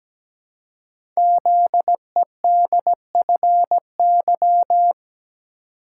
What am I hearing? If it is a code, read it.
Morse code, heard as a keyed tone, about 17 wpm, 700 Hz. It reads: ZEDFY
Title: Code Groups - 5 Letters - 17wpm